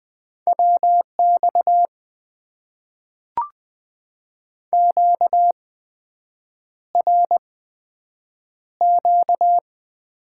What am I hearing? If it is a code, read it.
Morse code, heard as a keyed tone, about 20 wpm, spaced as for 7 wpm, 700 Hz. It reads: WX QRQ
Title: Top 90 QSO Elements - 8x Character Spacing - 20wpm